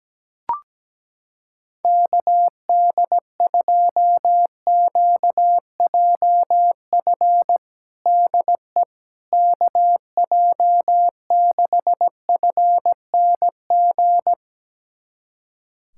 Morse code, heard as a keyed tone, about 17 wpm, 700 Hz. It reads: KD2QJF DE KJ6FNG
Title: USA Callsign DE Callsign - 6 Characters - 17wpm